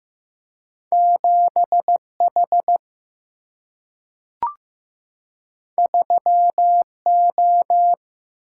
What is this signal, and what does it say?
Morse code, heard as a keyed tone, about 15 wpm, 700 Hz. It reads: 7 H 3 O
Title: Single-Letter Single-Number 15wpm